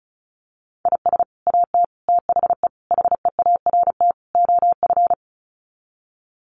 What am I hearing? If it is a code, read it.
Morse code, heard as a keyed tone, about 34 wpm, 700 Hz. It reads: IS AT THE HEART OF
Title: Sets of 5 Words Encore2 35wpm